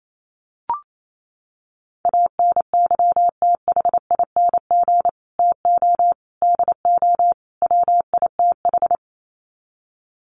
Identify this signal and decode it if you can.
Morse code, heard as a keyed tone, about 28 wpm, 700 Hz. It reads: ANYTHING TO DO WITH
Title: Sets of 4 Words 28wpm